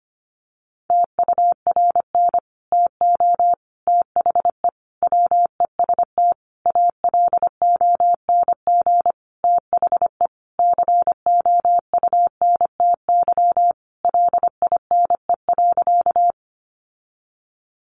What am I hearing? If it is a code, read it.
Morse code, heard as a keyed tone, about 25 wpm, 700 Hz. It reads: TURN TO THE WEST ALONG THE COUNTY LINE.
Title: Sentences from Top 1200 Words 25wpm